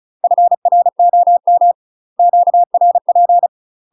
Morse code, heard as a keyed tone, about 34 wpm, 700 Hz. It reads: FROM QRP
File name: UK MASTER QSOs TRAINING / 35WPM / UK MASTER QSOs 0018